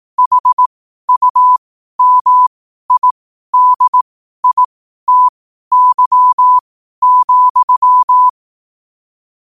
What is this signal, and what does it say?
Morse code, heard as a keyed tone, about 18 wpm, spaced as for 12 wpm, 1000 Hz. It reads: HUMIDITY,
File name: 12 wpm